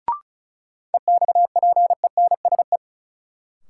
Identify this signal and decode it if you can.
Morse code, heard as a keyed tone, about 36 wpm, 700 Hz. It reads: EXPENSE
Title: Sets of 4 Words - New Word Drill - 35wpm